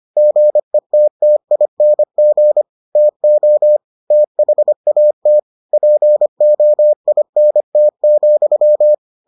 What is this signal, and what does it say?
Morse code, heard as a keyed tone, about 25 wpm, 600 Hz. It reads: GETTING TO THAT POINT,